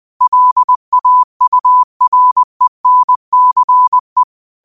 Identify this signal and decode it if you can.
Morse code, heard as a keyed tone, about 20 wpm, 1000 Hz. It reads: LAURENCE
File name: qotdfiles / 20wpm